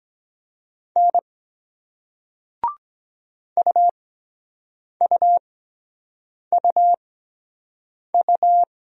Morse code, heard as a keyed tone, about 24 wpm, spaced as for 7 wpm, 700 Hz. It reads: N UUUU